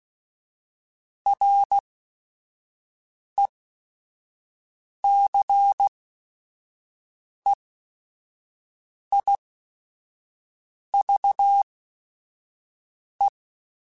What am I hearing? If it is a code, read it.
Morse code, heard as a keyed tone, about 16 wpm, spaced as for 5 wpm, 800 Hz. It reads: RECEIVE
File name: Quotes / Quotes-mp3s / Advice @5